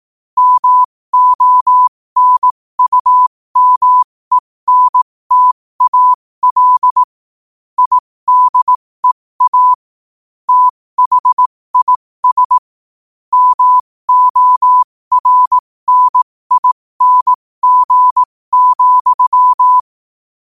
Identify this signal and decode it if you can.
Morse code, heard as a keyed tone, about 18 wpm, spaced as for 15 wpm, 1000 Hz. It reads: MONUMENTAL IDEA THIS MORNING,
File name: qotdfiles / 15wpm